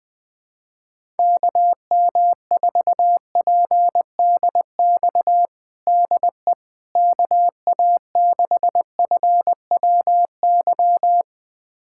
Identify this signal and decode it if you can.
Morse code, heard as a keyed tone, about 20 wpm, 700 Hz. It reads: KM4PDX DE KA6FWY
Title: USA Callsign DE Callsign - 6 Characters - 20wpm